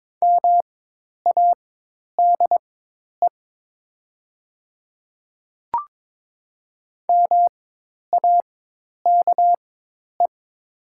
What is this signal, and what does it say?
Morse code, heard as a keyed tone, about 22 wpm, spaced as for 10 wpm, 700 Hz. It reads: MADE MAKE
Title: 4x Character Spacing - Top 200 Words - 22wpm